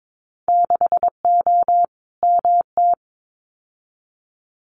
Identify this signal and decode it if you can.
Morse code, heard as a keyed tone, about 22 wpm, 700 Hz. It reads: 6O MT
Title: Winter Field Day Exchanges Encore 22wpm